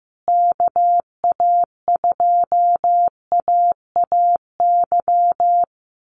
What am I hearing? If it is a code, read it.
Morse code, heard as a keyed tone, about 15 wpm, 700 Hz. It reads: KA2AAY
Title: USA Call Signs Repeated 3x 15wpm